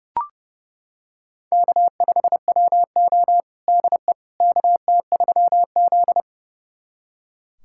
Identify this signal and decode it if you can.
Morse code, heard as a keyed tone, about 30 wpm, 700 Hz. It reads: K5WO DE KT3Z
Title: USA Callsign DE Callsign - 4 Characters - 30wpm